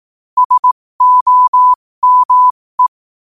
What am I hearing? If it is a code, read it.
Morse code, heard as a keyed tone, about 18 wpm, spaced as for 15 wpm, 1000 Hz. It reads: SOME
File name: qotdfiles / 15wpm